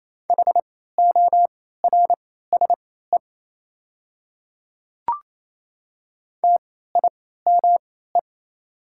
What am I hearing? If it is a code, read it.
Morse code, heard as a keyed tone, about 28 wpm, spaced as for 16 wpm, 700 Hz. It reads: HORSE TIME
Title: 3x Character Spacing - Top 300 Words - 28wpm